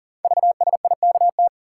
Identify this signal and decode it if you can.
Morse code, heard as a keyed tone, about 43 wpm, 690 Hz. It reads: VSIKT
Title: Code Groups - 5 Letters - 40wpm